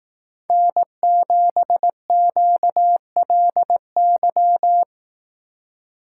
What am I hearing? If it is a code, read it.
Morse code, heard as a keyed tone, about 18 wpm, 700 Hz. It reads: N7QLY
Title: Speed Racing - USA 1x3 Call Signs 15wpm